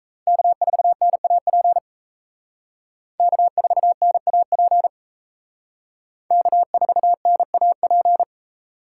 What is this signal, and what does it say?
Morse code, heard as a keyed tone, about 35 wpm, 700 Hz. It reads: K4NAP K4NAP K4NAP